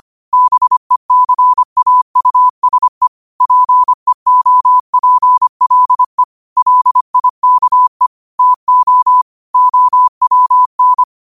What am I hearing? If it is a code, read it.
Morse code, heard as a keyed tone, about 25 wpm, 1000 Hz. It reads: BECAUSE PEOPLE LIKE TO OWN